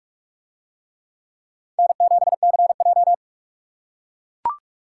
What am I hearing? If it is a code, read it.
Morse code, heard as a keyed tone, about 44 wpm, 700 Hz. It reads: N7CJ